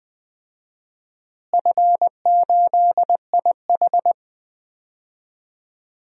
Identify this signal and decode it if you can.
Morse code, heard as a keyed tone, about 20 wpm, 700 Hz. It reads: F8IH
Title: DXCC-100 Call Signs - 4 Characters - Repeated 3x - 20wpm